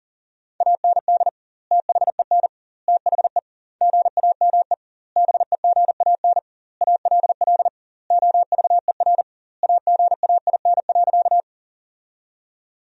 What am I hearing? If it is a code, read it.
Morse code, heard as a keyed tone, about 40 wpm, 700 Hz. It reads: AND THEN THE GAME BEGAN ALL OVER AGAIN.
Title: 2x Word-Spacing - Sentences from Top 500 Words - 40wpm